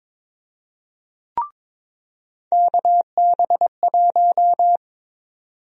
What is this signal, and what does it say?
Morse code, heard as a keyed tone, about 22 wpm, 700 Hz. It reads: KB1
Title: USA Call Sign Prefixes - 22wpm